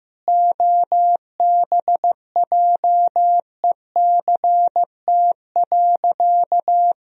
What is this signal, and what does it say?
Morse code, heard as a keyed tone, about 16 wpm, 700 Hz. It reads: OBJECT.
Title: Sentences from Top 1000 Words - Review Part 3 - 15wpm